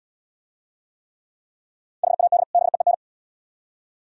Morse code, heard as a keyed tone, about 71 wpm, 700 Hz. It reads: HAD BEEN